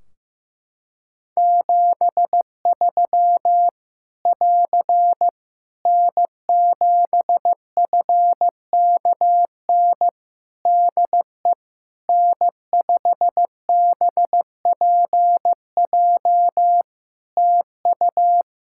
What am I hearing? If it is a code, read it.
Morse code, heard as a keyed tone, about 15 wpm, 700 Hz. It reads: 73 <AR> N7FKN DE N5BPJ TU
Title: Practice QSO - Exchange 4 - Ending - 15wpm